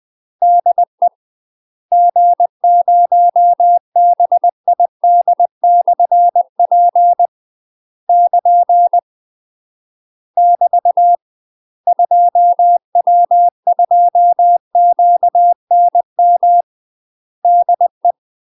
Morse code, heard as a keyed tone, about 20 wpm, 700 Hz. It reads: DE G0BID/P <KN> <BT> 2W2QNM DE
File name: UK MASTER QSOs TRAINING / 20WPM / UK MASTER QSOs 0015